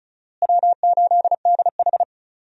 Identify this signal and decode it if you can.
Morse code, heard as a keyed tone, about 36 wpm, 700 Hz. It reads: W8DH
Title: USA Call Signs - 4 Characters - Repeated 2x - 35wpm